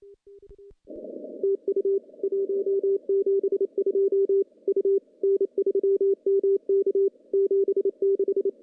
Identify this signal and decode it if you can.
Morse code, heard as a keyed tone, about 26 wpm, 390 Hz. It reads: TU 172 U N3MK 76
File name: contests / CW-SS-k5tr- -0400